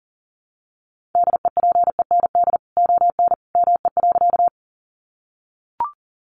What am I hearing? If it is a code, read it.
Morse code, heard as a keyed tone, about 40 wpm, 700 Hz. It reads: DEPEND ON ME.
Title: Sentences from Top 1300 Words 40wpm